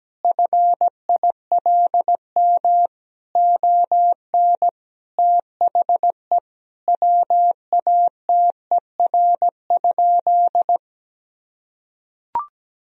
Morse code, heard as a keyed tone, about 17 wpm, 700 Hz. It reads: FILM ON THE WATER?
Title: Sentences from Top 1000 Words - Review Part 3 - 17wpm